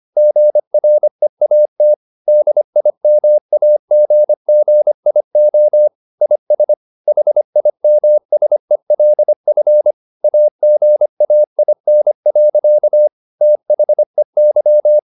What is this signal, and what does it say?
Morse code, heard as a keyed tone, about 25 wpm, 600 Hz. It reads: GREAT DIMAGGIO IS HIMSELF AGAIN. THEY
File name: the old man and the sea part 1 25wpm